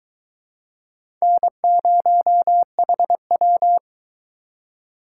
Morse code, heard as a keyed tone, about 20 wpm, 700 Hz. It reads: N0HW